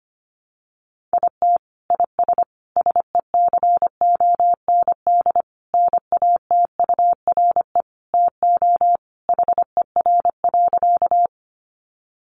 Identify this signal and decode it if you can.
Morse code, heard as a keyed tone, about 25 wpm, 700 Hz. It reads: IT IS SECOND NATURE TO HER.
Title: Sentences from Top 600 Words 25wpm